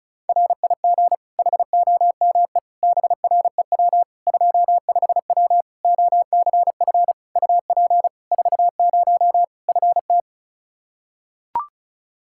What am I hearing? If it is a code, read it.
Morse code, heard as a keyed tone, about 34 wpm, 700 Hz. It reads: RIG HOME BREW 25W OCF UP 40 FT
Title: Practice QSO - Exchange 2 - Info - 35wpm